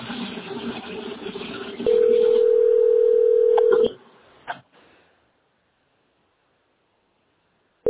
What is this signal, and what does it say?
Ringback tone (North America), 440 + 480 Hz.